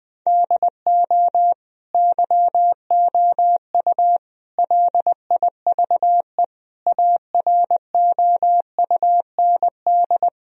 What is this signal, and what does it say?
Morse code, heard as a keyed tone, about 20 wpm, 700 Hz. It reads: DO YOU LIVE AROUND
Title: Sentences from Top 1000 Words - Review Part 3 - 20wpm